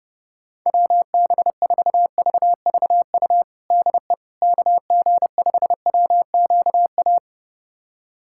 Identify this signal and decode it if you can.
Morse code, heard as a keyed tone, about 30 wpm, 700 Hz. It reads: WB4VVU DE KG5WQA
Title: USA Callsign DE Callsign - 6 Characters - 30wpm